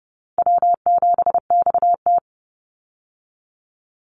Morse code, heard as a keyed tone, about 30 wpm, 700 Hz. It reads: W7XT
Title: Easy Call Signs - 30wpm